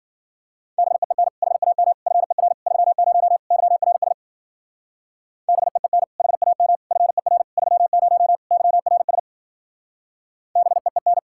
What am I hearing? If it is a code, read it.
Morse code, heard as a keyed tone, about 49 wpm, 700 Hz. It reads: BEEN HAM FER 30 YRS BEEN HAM FER 30 YRS BEEN
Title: Speed-Racing Common QSO Phrases - 40wpm